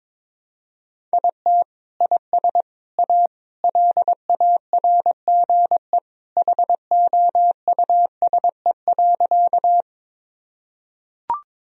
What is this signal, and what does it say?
Morse code, heard as a keyed tone, about 22 wpm, 700 Hz. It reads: IT IS A LARGE HOUSE.
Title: Sentences from Top 1000 Words - Review Part 1 - 22wpm